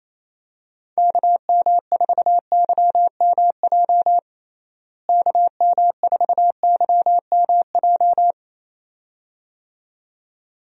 Morse code, heard as a keyed tone, about 28 wpm, 700 Hz. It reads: KM4YMJ KM4YMJ
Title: USA Call Signs - 6 Characters - Repeated 2x - 28wpm